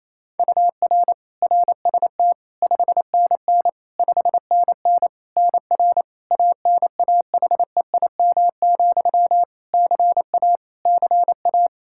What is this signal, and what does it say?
Morse code, heard as a keyed tone, about 27 wpm, 700 Hz. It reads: UR RST 5NN 5NN NR ANAHEIM, CA CA
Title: Practice QSO - Exchange 1 - Beginning - 28wpm